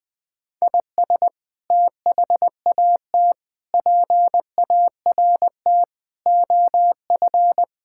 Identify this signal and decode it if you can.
Morse code, heard as a keyed tone, about 21 wpm, 700 Hz. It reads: IS THAT PART OF
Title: Sentences from Top 1000 Words - Review Part 5 - 20wpm